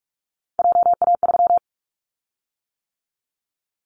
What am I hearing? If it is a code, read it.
Morse code, heard as a keyed tone, about 46 wpm, 700 Hz. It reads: JA3
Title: Callsign Prefixes for Top 50 DXCC - 45wpm